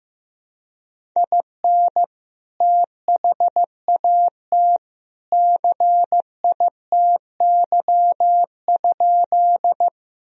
Morse code, heard as a keyed tone, about 16 wpm, 700 Hz. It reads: IN THAT CITY?
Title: Sentences from Top 1000 Words - Review Part 1 - 15wpm